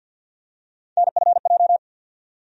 Morse code, heard as a keyed tone, about 49 wpm, 700 Hz. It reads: NFJ
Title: USA Call Sign Postfixes - 3 Letters 50wpm